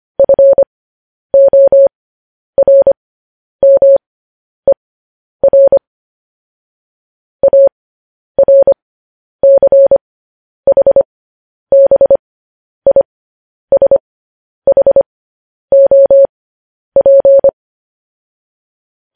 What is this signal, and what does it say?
Morse code, heard as a keyed tone, about 25 wpm, spaced as for 10 wpm, 550 Hz. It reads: FORMER ARCHBISHOP
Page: news-headlines-in-morse-code-at-10-wpm